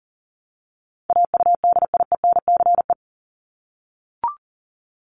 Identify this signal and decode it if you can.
Morse code, heard as a keyed tone, about 40 wpm, 700 Hz. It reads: AUDIENCE
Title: Top 751-1000 Words 40wpm